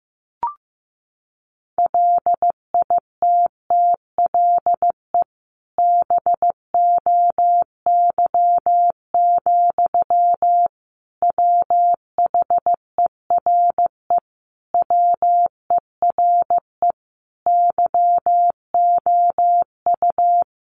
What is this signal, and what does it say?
Morse code, heard as a keyed tone, about 15 wpm, 700 Hz. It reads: LITTLE BOY, WHERE WERE YOU
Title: Sentences from Top 1000 Words 15wpm